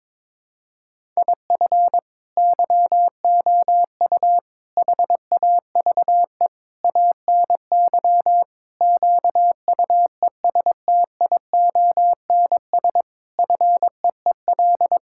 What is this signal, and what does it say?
Morse code, heard as a keyed tone, about 21 wpm, 700 Hz. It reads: IF YOU HAVE ANY QUESTIONS FEEL
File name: practice / 22wpm / Sentences with 8-Word Phrases - 22wpm